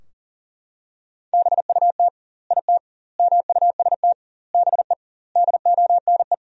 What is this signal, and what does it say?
Morse code, heard as a keyed tone, about 40 wpm, 700 Hz. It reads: BUT IT MUST BE DONE
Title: 2x Word-Spacing - Sentences from Top 500 Words - 40wpm